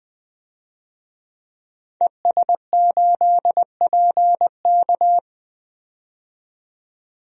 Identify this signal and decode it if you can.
Morse code, heard as a keyed tone, about 20 wpm, 700 Hz. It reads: ES8PK
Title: DXCC-100 Call Signs - 5 Characters - 20wpm